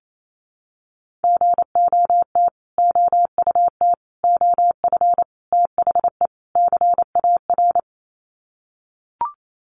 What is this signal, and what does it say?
Morse code, heard as a keyed tone, about 28 wpm, 700 Hz. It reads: GOT OUT OF THE CAR
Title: Sets of 5 Words Encore 28wpm